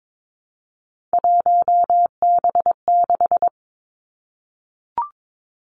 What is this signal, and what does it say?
Morse code, heard as a keyed tone, about 22 wpm, 700 Hz. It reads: 1B6
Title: Three Challenging Letter-Number Combinations 22wpm